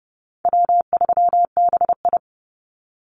Morse code, heard as a keyed tone, about 30 wpm, 700 Hz. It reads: W3BI
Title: Call Signs Repeated 2x - 30wpm